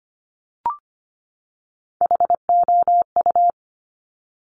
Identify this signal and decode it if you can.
Morse code, heard as a keyed tone, about 25 wpm, 700 Hz. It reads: HOU
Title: Top Three Letter Combinations in Words 25wpm